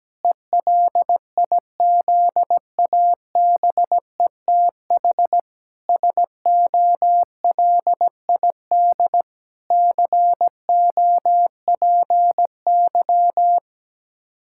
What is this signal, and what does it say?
Morse code, heard as a keyed tone, about 17 wpm, 700 Hz. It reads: ELIZABETH SOLID COPY
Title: Practice QSO - Exchange 3 - More Info - 17wpm